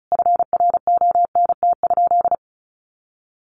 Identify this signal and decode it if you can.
Morse code, heard as a keyed tone, about 34 wpm, 700 Hz. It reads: FRONT?
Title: Sentences from Top 1400 Words 35wpm